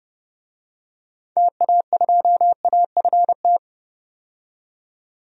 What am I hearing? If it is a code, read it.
Morse code, heard as a keyed tone, about 30 wpm, 700 Hz. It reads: TA2AFT
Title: DXCC-100 Call Signs - 6 Characters - 30wpm